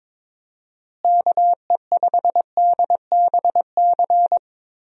Morse code, heard as a keyed tone, about 22 wpm, 700 Hz. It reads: KE5DBC
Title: USA Call Signs - 6 Characters - Repeated 2x - 22wpm